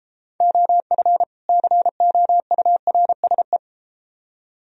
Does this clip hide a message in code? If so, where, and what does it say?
Yes: Morse code from 0.4 to 3.6 s, heard as a keyed tone, about 33 wpm, 700 Hz. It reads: OF COURSE